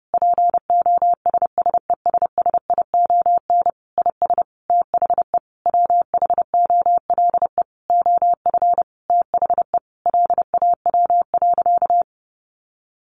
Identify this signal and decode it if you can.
Morse code, heard as a keyed tone, about 30 wpm, 700 Hz. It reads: POSSESSION IS THE WHOLE OF THE LAW.
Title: Sentences from Top 1400 Words 30wpm